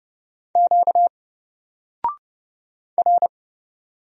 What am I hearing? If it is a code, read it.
Morse code, heard as a keyed tone, about 29 wpm, 700 Hz. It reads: Q R
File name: Single-Letter-Number-30wpm